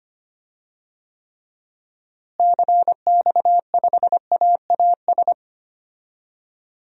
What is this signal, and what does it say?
Morse code, heard as a keyed tone, about 25 wpm, 700 Hz. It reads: CX5AAS
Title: DXCC-100 Call Signs - 6 Characters - Repeated 3x - 25wpm